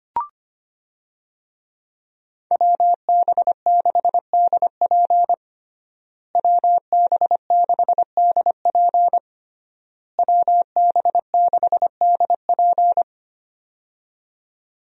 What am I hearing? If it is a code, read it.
Morse code, heard as a keyed tone, about 25 wpm, 700 Hz. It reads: WB6DP WB6DP WB6DP
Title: USA Call Signs - 5 Characters - Repeated 3x - 25wpm